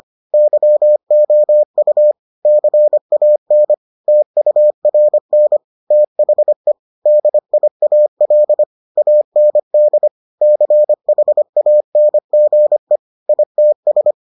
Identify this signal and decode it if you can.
Morse code, heard as a keyed tone, about 25 wpm, 600 Hz. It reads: YOU CAN TURN THE DIAL AND CHANGE ITS